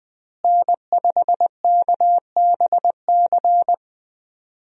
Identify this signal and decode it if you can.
Morse code, heard as a keyed tone, about 21 wpm, 700 Hz. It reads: N5KBC
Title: USA Call Signs - 5 Characters - Repeated 3x - 20wpm